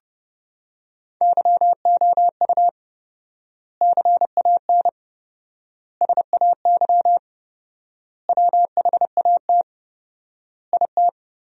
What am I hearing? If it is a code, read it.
Morse code, heard as a keyed tone, about 30 wpm, 700 Hz. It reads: YOU CAN SAY WHAT IT
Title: 4x Word-Spacing - Sentences from Top 200 Words - 30wpm